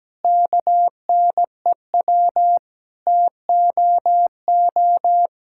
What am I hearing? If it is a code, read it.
Morse code, heard as a keyed tone, about 17 wpm, 700 Hz. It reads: KNEW TOO
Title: Sentences from Top 1000 Words - Review Part 5 - 17wpm